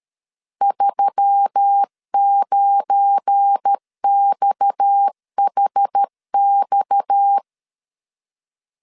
Morse code, heard as a keyed tone, about 13 wpm, 800 Hz. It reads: 39XHX
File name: LEZIONI / 9 2 12wpm 2301